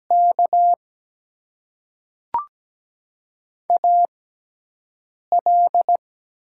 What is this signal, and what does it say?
Morse code, heard as a keyed tone, about 17 wpm, spaced as for 4 wpm, 700 Hz. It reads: KAL
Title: USA Call Signs - 4 Characters - 6x Character Spacing - 17wpm